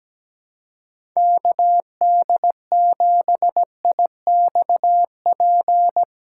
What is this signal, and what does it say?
Morse code, heard as a keyed tone, about 17 wpm, 700 Hz. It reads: KD7IXP
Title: USA Callsign DE Callsign 17wpm